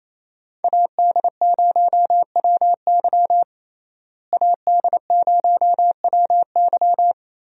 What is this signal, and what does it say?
Morse code, heard as a keyed tone, about 27 wpm, 700 Hz. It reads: AD0WY AD0WY
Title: USA Call Signs - 5 Characters - Repeated 3x - 28wpm